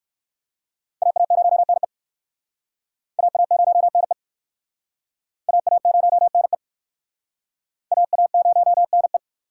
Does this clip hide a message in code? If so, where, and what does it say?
Yes: Morse code from 1.0 to 9.2 s, heard as a keyed tone, about 60 wpm, 700 Hz. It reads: AA0NE AA0NE AA0NE AA0NE